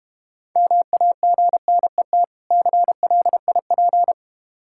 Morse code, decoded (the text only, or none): MAGNET CLIP